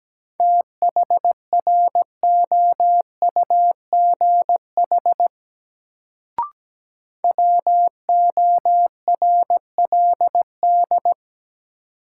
Morse code, heard as a keyed tone, about 17 wpm, 700 Hz. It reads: THROUGH WORLD
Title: Sentences from Top 200 Words - New Word Drill - 17wpm